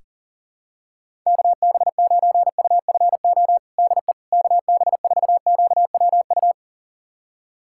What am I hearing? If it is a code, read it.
Morse code, heard as a keyed tone, about 40 wpm, 700 Hz. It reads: KB9UFO DE KB4QWU
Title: USA Callsign DE Callsign - 6 Characters - 40wpm